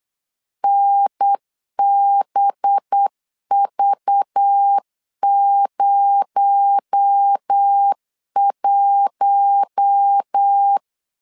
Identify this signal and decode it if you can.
Morse code, heard as a keyed tone, about 8 wpm, 800 Hz. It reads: NBV01
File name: LEZIONI / 6 PKX0 / 6 PKX0 8wpm 3341